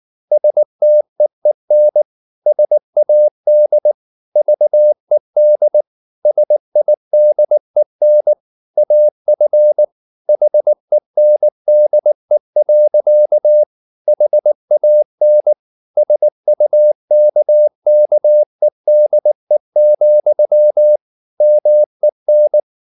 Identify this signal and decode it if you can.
Morse code, heard as a keyed tone, about 19 wpm, 600 Hz. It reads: STEEN SAD VED SIDEN AF HENDE. HAN SUKKEDE, MEN
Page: Ferie 19 wpm